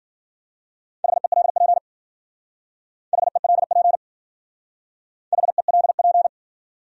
Morse code, heard as a keyed tone, about 50 wpm, 700 Hz. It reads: HELP HELP HELP